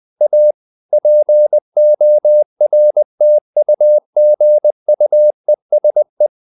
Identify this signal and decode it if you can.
Morse code, heard as a keyed tone, about 20 wpm, 600 Hz. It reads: A PORTUGUESE
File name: the old man and the sea / the old man and the sea part 1 20wpm